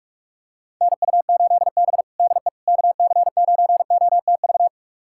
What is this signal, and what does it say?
Morse code, heard as a keyed tone, about 46 wpm, 700 Hz. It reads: NU8B DE KC9OTV
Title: USA Callsign DE Callsign 45wpm